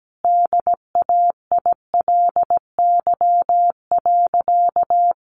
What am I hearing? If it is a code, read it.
Morse code, heard as a keyed tone, about 17 wpm, 700 Hz. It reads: DAILY.
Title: Sentences from Top 1100 Words 17wpm